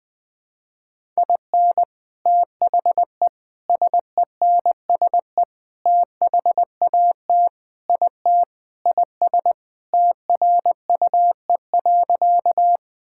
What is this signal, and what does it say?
Morse code, heard as a keyed tone, about 20 wpm, 700 Hz. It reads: IN THE SENSE THAT IT IS TRUE.
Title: Sentences with 4-Word Phrases - 20wpm